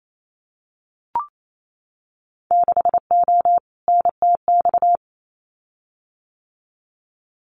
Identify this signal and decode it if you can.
Morse code, heard as a keyed tone, about 28 wpm, 700 Hz. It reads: 6O NTX
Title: Winter Field Day Exchanges Encore 28wpm